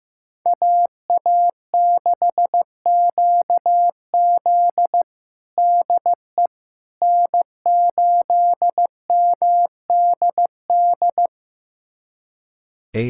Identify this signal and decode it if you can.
Morse code, heard as a keyed tone, about 15 wpm, 700 Hz. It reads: AA6QZ DE N8MDD
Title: USA Callsign DE Callsign - 5 Characters - 15wpm